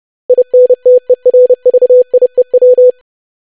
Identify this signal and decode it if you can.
Morse code, heard as a keyed tone, about 30 wpm, 500 Hz. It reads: INTERVIEW
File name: lesson / web / longwords-25wpm